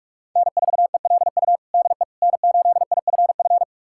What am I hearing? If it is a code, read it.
Morse code, heard as a keyed tone, about 46 wpm, 700 Hz. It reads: N4ELU DE N8IFF
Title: USA Callsign DE Callsign - 5 Characters - 45wpm